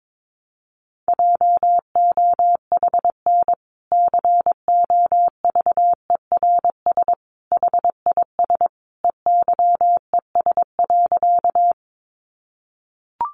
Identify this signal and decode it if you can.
Morse code, heard as a keyed tone, about 22 wpm, 700 Hz. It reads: JOHN COVERS HIS EYES.
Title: Easy Phrases 22wpm